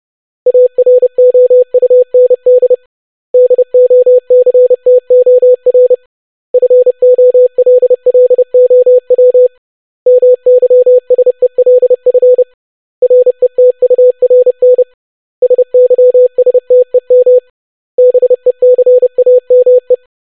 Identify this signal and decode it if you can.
Morse code, heard as a keyed tone, about 30 wpm, 500 Hz. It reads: AROUND DOCTOR FOLLOW MYSELF RETURN SYSTEM BECAME
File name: basic500-25wpm